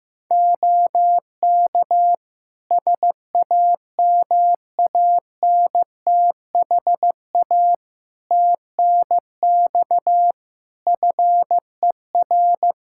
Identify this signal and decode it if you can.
Morse code, heard as a keyed tone, about 15 wpm, 700 Hz. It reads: OK SAMANTHA TNX FER
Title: Practice QSO - Exchange 4 - Ending - 15wpm